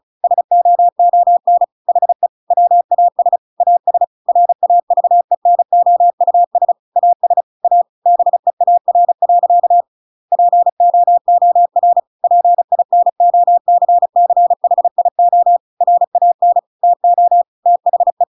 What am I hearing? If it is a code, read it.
Morse code, heard as a keyed tone, about 34 wpm, 700 Hz. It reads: SOON HE WAS AS RAVENOUS AS A BEAR. POOR PINOCCHIO RAN TO THE
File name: Pinocchio / Pinocchio ch5 35 35 ik1hge